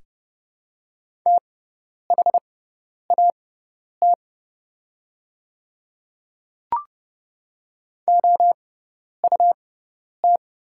Morse code, heard as a keyed tone, about 29 wpm, spaced as for 10 wpm, 690 Hz. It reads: THAT OUT
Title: 6x Character Spacing - Top 200 Words - 30wpm